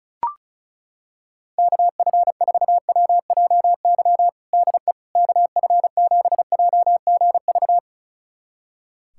Morse code, heard as a keyed tone, about 36 wpm, 700 Hz. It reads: KF4WJY DE KF7JGV
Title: USA Callsign DE Callsign 35wpm